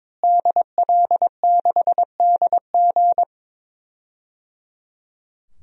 Morse code, heard as a keyed tone, about 22 wpm, 700 Hz. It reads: DL6DG
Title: DXCC-100 Call Signs - 5 Characters - Repeated 3x - 22wpm